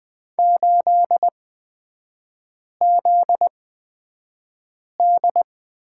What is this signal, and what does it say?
Morse code, heard as a keyed tone, about 20 wpm, spaced as for 5 wpm, 700 Hz. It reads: 8ZD
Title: 243 Lesson 39e - Letter z - Three Characters 20wpm